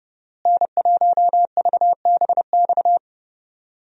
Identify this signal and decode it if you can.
Morse code, heard as a keyed tone, about 30 wpm, 700 Hz. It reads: N1VBX
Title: USA Call Signs - 5 Characters - Repeated 2x - 30wpm